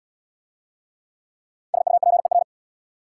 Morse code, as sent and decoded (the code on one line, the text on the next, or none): ... ..- .--. .-.
SUPER